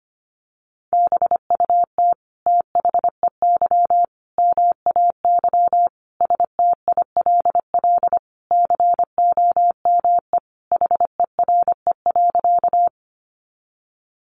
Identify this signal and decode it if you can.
Morse code, heard as a keyed tone, about 25 wpm, 700 Hz. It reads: BUT THEY MAY STILL COME HERE.
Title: Sentences from Top 300 Words 25wpm